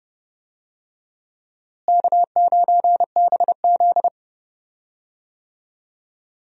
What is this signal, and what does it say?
Morse code, heard as a keyed tone, about 30 wpm, 700 Hz. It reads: K9BZ
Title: USA Call Signs - 4 Characters - Repeated 3x - 30wpm